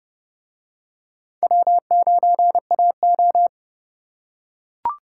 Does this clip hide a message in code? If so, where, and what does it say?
Yes: Morse code from 1.4 to 3.5 s, heard as a keyed tone, about 30 wpm, 700 Hz. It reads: W9AO